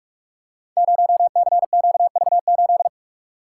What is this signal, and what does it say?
Morse code, heard as a keyed tone, about 44 wpm, 700 Hz. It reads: 0CQV8
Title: Code Groups - 5 Letter:Numbers - 45wpm